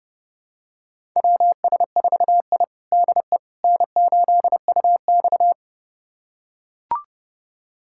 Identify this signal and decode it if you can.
Morse code, heard as a keyed tone, about 31 wpm, 700 Hz. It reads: WS4I DE N8UX
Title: USA Callsign DE Callsign - 4 Characters - 30wpm